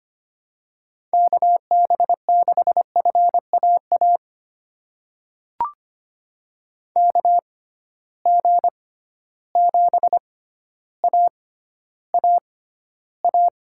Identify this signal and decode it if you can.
Morse code, heard as a keyed tone, about 25 wpm, spaced as for 13 wpm, 700 Hz. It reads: KB6FAA KG7AAA
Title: USA Call Signs - 6 Characters - 6x Character Spacing - 25wpm